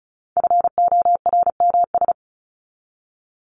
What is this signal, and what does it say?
Morse code, heard as a keyed tone, about 34 wpm, 700 Hz. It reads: FORMS
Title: ES:S Suffix - Top 100 Words with Base Words 35wpm